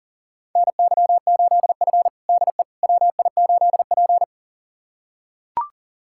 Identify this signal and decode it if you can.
Morse code, heard as a keyed tone, about 40 wpm, 700 Hz. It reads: NY8F DE WI8P
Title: USA Callsign DE Callsign - 4 Characters - 40wpm